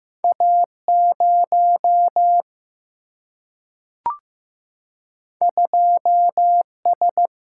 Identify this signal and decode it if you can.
Morse code, heard as a keyed tone, about 15 wpm, 700 Hz. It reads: A 0 2 S
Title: Single-Letter Single-Number 15wpm